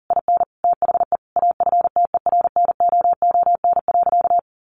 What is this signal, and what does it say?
Morse code, heard as a keyed tone, about 40 wpm, 700 Hz. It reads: IN THE AFTERNOON.
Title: Sentences from Top 1500 Words 40wpm